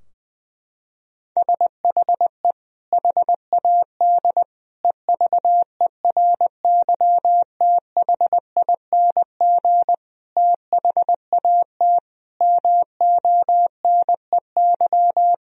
Morse code, heard as a keyed tone, about 20 wpm, 700 Hz. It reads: SHE HAD EVERYTHING THAT MONEY
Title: Sentences from Top 1000 Words - Review Part 4 - 20wpm